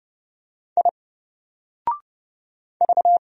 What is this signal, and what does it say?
Morse code, heard as a keyed tone, about 29 wpm, 700 Hz. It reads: I V
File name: Single-Letter-Number-30wpm